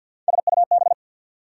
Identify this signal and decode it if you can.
Morse code, heard as a keyed tone, about 48 wpm, 700 Hz. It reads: SUB
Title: Prefixes and Suffixes 50wpm